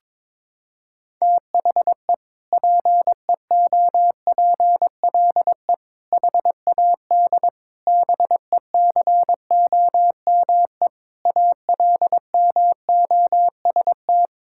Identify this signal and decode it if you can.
Morse code, heard as a keyed tone, about 22 wpm, 700 Hz. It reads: THE PEOPLE HAD BECOME ALMOST
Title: Sentences from Top 1000 Words - Review Part 6 - 22wpm